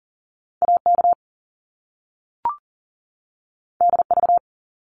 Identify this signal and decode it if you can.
Morse code, heard as a keyed tone, about 40 wpm, 700 Hz. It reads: AK DV
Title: Top Two Letter Combinations in Words 40wpm